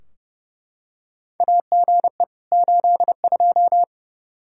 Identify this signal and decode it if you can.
Morse code, heard as a keyed tone, about 30 wpm, 700 Hz. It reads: AGE 82
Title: K1USN SST QSO Practice 30wpm